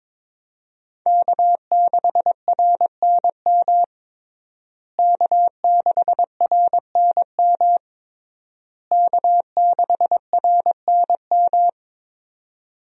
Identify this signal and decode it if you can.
Morse code, heard as a keyed tone, about 22 wpm, 700 Hz. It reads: K6RNM K6RNM K6RNM
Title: USA Call Signs - 5 Characters - Repeated 3x - 22wpm